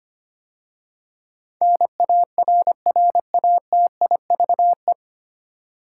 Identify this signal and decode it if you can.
Morse code, heard as a keyed tone, about 25 wpm, 700 Hz. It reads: NARRATIVE
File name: practice / 25wpm / 9 Letter Words - First 3 Letters Given - 25wpm